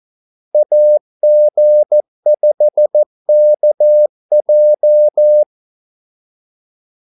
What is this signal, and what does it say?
Morse code, heard as a keyed tone, about 14 wpm, 600 Hz. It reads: AG5KJ
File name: Callsigns-14wpm